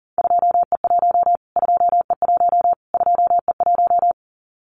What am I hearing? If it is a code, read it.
Morse code, heard as a keyed tone, about 40 wpm, 700 Hz. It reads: 2E1 2E1 2E1
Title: Callsign Prefixes for Top 25 DXCC - Repeated 3x - 40wpm